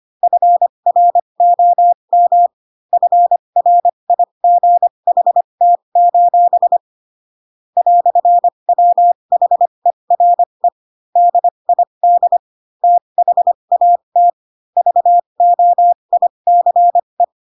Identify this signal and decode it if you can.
Morse code, heard as a keyed tone, about 25 wpm, spaced as for 22 wpm, 700 Hz. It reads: FROM FRIGHT: "WHERE DID THAT VOICE